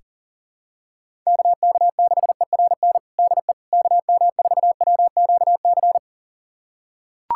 Morse code, heard as a keyed tone, about 40 wpm, 700 Hz. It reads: KK6ERN DE KM4WQC
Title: USA Callsign DE Callsign - 6 Characters - 40wpm